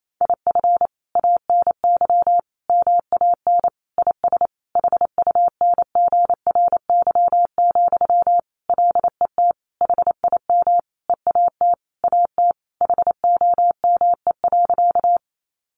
Morse code, heard as a keyed tone, about 28 wpm, 700 Hz. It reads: IF ANY MAN IS HUNGRY, LET HIM EAT AT HOME.
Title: Sentences from Top 1400 Words 28wpm